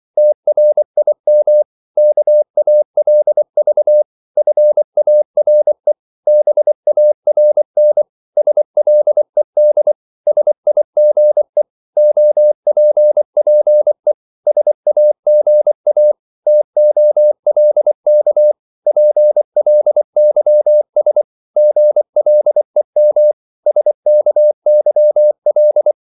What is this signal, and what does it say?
Morse code, heard as a keyed tone, about 24 wpm, 600 Hz. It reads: TRIM KALV FARE BARN SLED SIGE OPPE SAGA TOLK PLYS GLEM SKYL
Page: Korte danske ord 24wpm